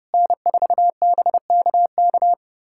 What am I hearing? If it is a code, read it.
Morse code, heard as a keyed tone, about 31 wpm, 700 Hz. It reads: N4BKK
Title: USA Callsign DE Callsign - 5 Characters - 30wpm